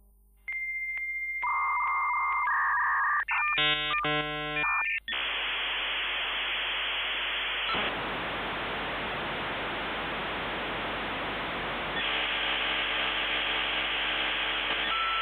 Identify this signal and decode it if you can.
Dial-up modem handshake.
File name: Marie / son / modem56k